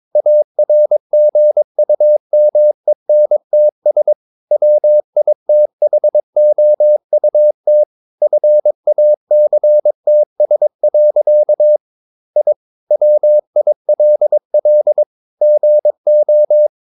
Morse code, heard as a keyed tone, about 22 wpm, 600 Hz. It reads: ARGUMENTS WITHOUT FACTS. I WILL GO
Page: Romanen Never af Ken Follet på engelsk.